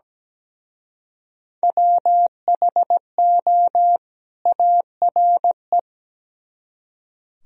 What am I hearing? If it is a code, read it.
Morse code, heard as a keyed tone, about 20 wpm, 700 Hz. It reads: WHO ARE
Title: Speed-Racing - Sets of 2 Words 17wpm